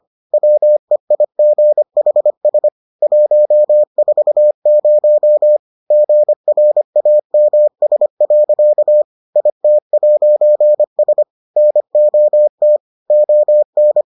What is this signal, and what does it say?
Morse code, heard as a keyed tone, about 25 wpm, 600 Hz. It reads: WEIGHS 140 GRAMS. IT'S NOT ON